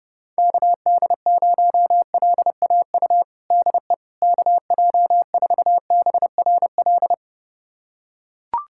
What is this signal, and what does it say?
Morse code, heard as a keyed tone, about 30 wpm, 700 Hz. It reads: KD0LAU DE KJ4BRL
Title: USA Callsign DE Callsign 30wpm